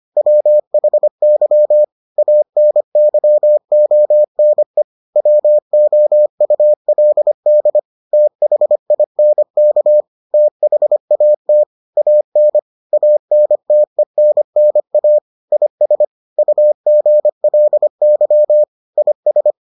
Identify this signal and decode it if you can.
Morse code, heard as a keyed tone, about 25 wpm, 600 Hz. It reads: WHY ANYONE WOULD THINK THAT AN ANTENNA IS UGLY IS